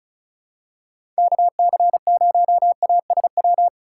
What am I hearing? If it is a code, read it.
Morse code, heard as a keyed tone, about 36 wpm, 700 Hz. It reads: KC0ASW
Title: USA Call Signs - 2x Character Spacing - 35wpm